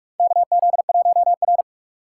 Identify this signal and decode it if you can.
Morse code, heard as a keyed tone, about 46 wpm, 700 Hz. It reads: KZ1R
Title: USA Call Signs - 4 Characters - 6x Character Spacing - 45wpm